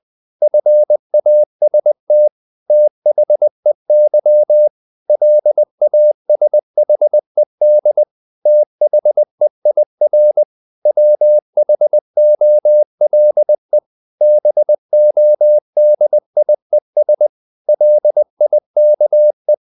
Morse code, heard as a keyed tone, about 20 wpm, 600 Hz. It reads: FAST THEY LASHED THEIR WHOLE BODIES LIKE
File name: the old man and the sea part 2 20wpm